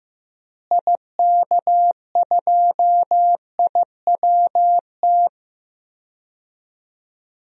Morse code, heard as a keyed tone, about 15 wpm, 700 Hz. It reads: IK2IWT
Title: DXCC-100 Call Signs - 6 Characters - Repeated 3x - 15wpm